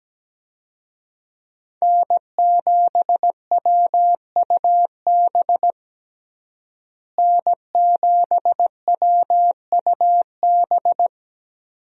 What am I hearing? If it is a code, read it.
Morse code, heard as a keyed tone, about 17 wpm, 700 Hz. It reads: N7WUB N7WUB
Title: USA Call Signs - 5 Characters - Repeated 2x - 17wpm